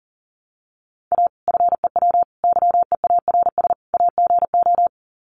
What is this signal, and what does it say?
Morse code, heard as a keyed tone, about 40 wpm, 700 Hz. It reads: A FEW YEARS AGO
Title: Sets of 4 Words 40wpm